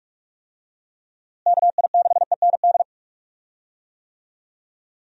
Morse code, heard as a keyed tone, about 41 wpm, 700 Hz. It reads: KI6END
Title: USA Call Signs - 6 Characters - Repeated 3x - 45wpm